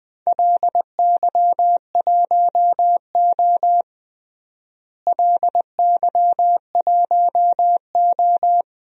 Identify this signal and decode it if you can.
Morse code, heard as a keyed tone, about 20 wpm, 700 Hz. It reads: LY1O LY1O
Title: DXCC-100 Call Signs - 4 Characters - Repeated 3x - 20wpm